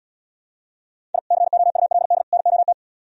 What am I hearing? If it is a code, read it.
Morse code, heard as a keyed tone, about 77 wpm, 700 Hz. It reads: I COULD NOT